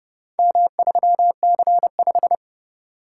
Morse code, heard as a keyed tone, about 30 wpm, 700 Hz. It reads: M3C5